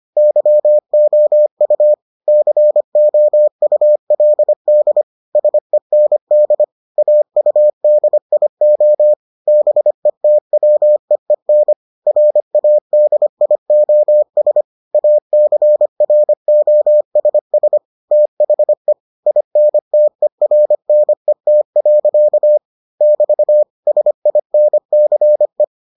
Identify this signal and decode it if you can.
Morse code, heard as a keyed tone, about 25 wpm, 600 Hz. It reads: YOU COULD SEND AUDIO BETWEEN RADIOS ACROSS THE INTERNET. <BT> SINCE